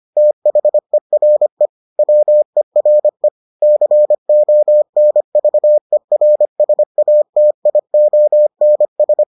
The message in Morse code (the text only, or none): THERE WERE CONVERSATIONS